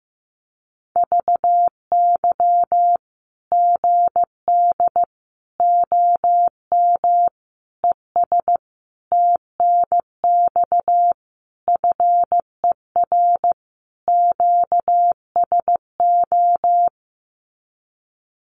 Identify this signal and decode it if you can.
Morse code, heard as a keyed tone, about 15 wpm, 700 Hz. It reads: VY GD OM ES TNX FER QSO
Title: Common QSO Phrases 15wpm